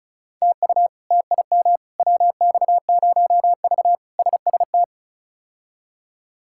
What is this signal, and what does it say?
Morse code, heard as a keyed tone, about 34 wpm, 700 Hz. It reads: TU TIM WX0V SST
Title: K1USN SST QSO Practice - 35wpm